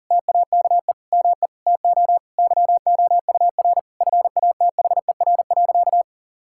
Morse code, heard as a keyed tone, about 40 wpm, 690 Hz. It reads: TAKE ME TO YOUR FATHER.
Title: Sentences from Top 1000 Words - Review Part 6 - 40wpm